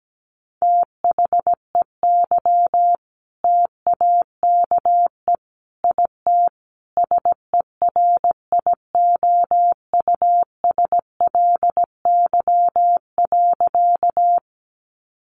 Morse code, heard as a keyed tone, about 17 wpm, 700 Hz. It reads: THEY TAKE IT SERIOUSLY.
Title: Sentences from Top 1200 Words 17wpm